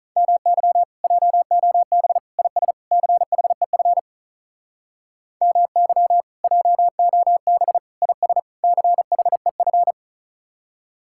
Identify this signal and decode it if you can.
Morse code, heard as a keyed tone, about 45 wpm, 700 Hz. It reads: MY JOB IS CHEF MY JOB IS CHEF